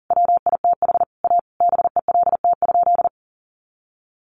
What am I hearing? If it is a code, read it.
Morse code, heard as a keyed tone, about 37 wpm, 700 Hz. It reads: WITH A BELT?
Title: Sentences from Top 1400 Words 40wpm